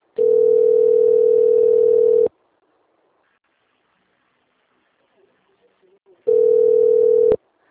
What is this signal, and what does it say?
Ringback tone (North America), 440 + 480 Hz.